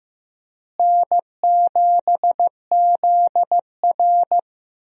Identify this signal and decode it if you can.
Morse code, heard as a keyed tone, about 15 wpm, 700 Hz. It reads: N7ZR
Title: USA Call Signs - 4 Characters - Repeated 2x - 15wpm